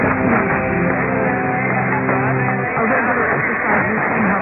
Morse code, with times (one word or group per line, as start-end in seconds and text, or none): none